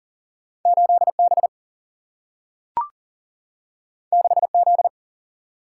Morse code, heard as a keyed tone, about 40 wpm, 700 Hz. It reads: 8B 6Z
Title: Single-Letter Single-Number 40wpm